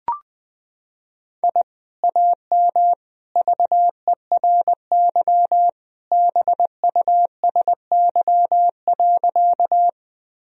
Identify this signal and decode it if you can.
Morse code, heard as a keyed tone, about 21 wpm, 700 Hz. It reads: I AM VERY BUSY.
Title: Sentences from Top 1000 Words - Review Part 3 - 20wpm